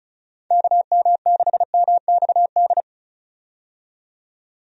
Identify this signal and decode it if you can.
Morse code, heard as a keyed tone, about 36 wpm, 700 Hz. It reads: KM6MXD
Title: USA Call Signs - 6 Characters - 35wpm